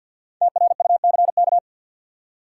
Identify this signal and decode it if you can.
Morse code, heard as a keyed tone, about 53 wpm, 700 Hz. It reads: TRUCK
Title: Five Letter Words - 50wpm